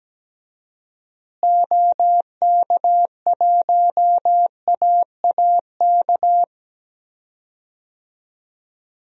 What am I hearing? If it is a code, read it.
Morse code, heard as a keyed tone, about 17 wpm, 700 Hz. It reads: OK1AAK
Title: DXCC-100 Call Signs - 6 Characters - Repeated 2x - 17wpm